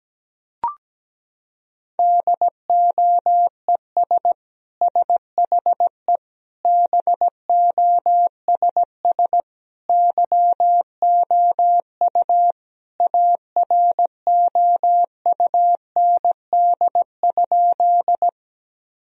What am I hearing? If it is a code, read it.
Morse code, heard as a keyed tone, about 17 wpm, 700 Hz. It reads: DOES SHE BOSS YOU AROUND?
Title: Sentences from Top 1000 Words - Review Part 4 - 17wpm